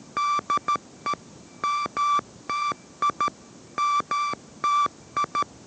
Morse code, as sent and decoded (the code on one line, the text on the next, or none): -.. / -- - .. / -- - ..
DE MTI MTI